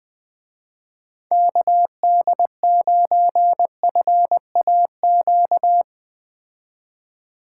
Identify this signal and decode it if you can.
Morse code, heard as a keyed tone, about 20 wpm, 700 Hz. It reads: KD9FAQ